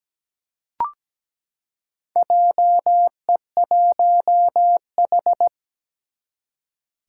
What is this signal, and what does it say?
Morse code, heard as a keyed tone, about 17 wpm, 700 Hz. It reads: JE1H
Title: Four Challenging Letter-Number Combinations 17wpm